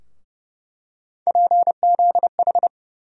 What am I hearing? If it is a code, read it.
Morse code, heard as a keyed tone, about 29 wpm, 700 Hz. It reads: PZH
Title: USA Call Sign Postfixes - 3 Letters 30wpm